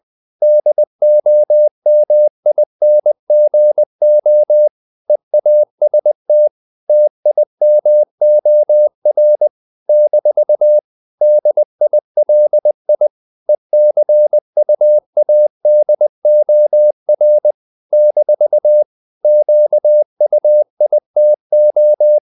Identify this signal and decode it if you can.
Morse code, heard as a keyed tone, about 20 wpm, 600 Hz. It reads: DOMINGO EAST TIMOR - DILI ECUADOR - QUITO
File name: CW / Practice / NationalCapitals20